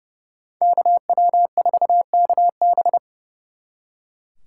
Morse code, heard as a keyed tone, about 28 wpm, 700 Hz. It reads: KW4KB
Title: Speed Racing - USA Call Signs 30wpm